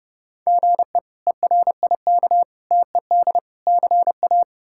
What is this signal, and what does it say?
Morse code, heard as a keyed tone, about 30 wpm, 700 Hz. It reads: GE ERIK TED CA
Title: K1USN SST QSO Practice - 30wpm